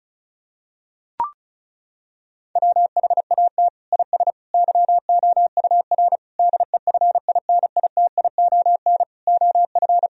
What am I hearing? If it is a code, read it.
Morse code, heard as a keyed tone, about 36 wpm, 700 Hz. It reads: WHAT IS YOUR DEFINITION OF
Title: Sentences from Top 1000 Words - Review Part 4 - 35wpm